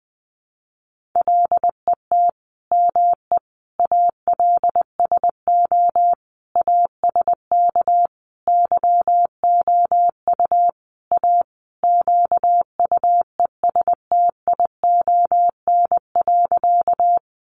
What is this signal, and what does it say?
Morse code, heard as a keyed tone, about 20 wpm, 700 Hz. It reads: LET ME ALSO ASK YOU A QUESTION.
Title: Sentences from Top 400 Words 20wpm